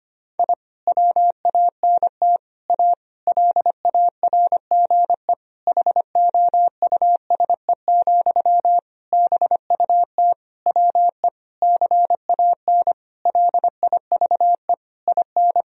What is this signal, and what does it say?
Morse code, heard as a keyed tone, about 25 wpm, 700 Hz. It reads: I WANT A LARGE HOUSE, BUT WE CAN LIVE IN
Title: Sentences from Top 1000 Words - Review Part 1 - 25wpm